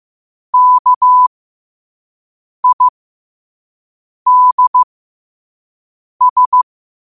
Morse code, heard as a keyed tone, about 15 wpm, spaced as for 5 wpm, 1000 Hz. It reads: KIDS
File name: qotdfiles / 5wpm